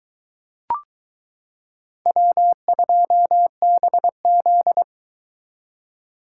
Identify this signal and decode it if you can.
Morse code, heard as a keyed tone, about 22 wpm, 700 Hz. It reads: W2BZ